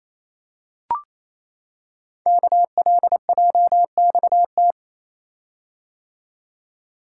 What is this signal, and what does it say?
Morse code, heard as a keyed tone, about 28 wpm, 700 Hz. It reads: KLJXT
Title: Code Groups - 5 Letter:Numbers - 28wpm